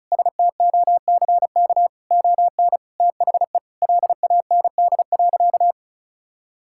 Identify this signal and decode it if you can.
Morse code, heard as a keyed tone, about 36 wpm, 700 Hz. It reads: STOCK ON THE LAND.
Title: Sentences from Top 1000 Words - Review Part 5 - 35wpm